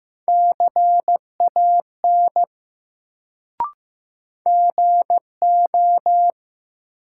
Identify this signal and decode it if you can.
Morse code, heard as a keyed tone, about 15 wpm, 700 Hz. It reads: CAN GO
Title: Sentences from Top 100 Words - New Word Drill - 15wpm